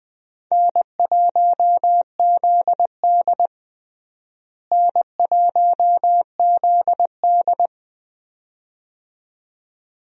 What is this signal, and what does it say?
Morse code, heard as a keyed tone, about 20 wpm, 700 Hz. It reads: N1ZD N1ZD
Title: USA Call Signs - 4 Characters - Repeated 2x - 20wpm